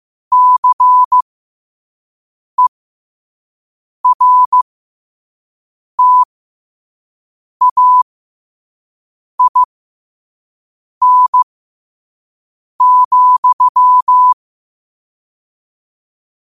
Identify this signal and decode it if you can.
Morse code, heard as a keyed tone, about 15 wpm, spaced as for 5 wpm, 1000 Hz. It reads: CERTAIN,
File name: qotdfiles / 5wpm